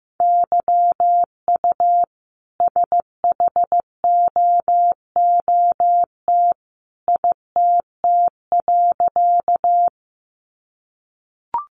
Morse code, heard as a keyed tone, about 15 wpm, 700 Hz. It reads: YU SHOOT ITT.
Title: A Sound Of Thunder 15wpm